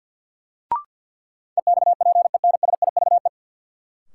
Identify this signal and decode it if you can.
Morse code, heard as a keyed tone, about 53 wpm, 700 Hz. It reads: EXPENSIVE
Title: Sentences from Top 1500 Words - New Word Drill Alt - 50wpm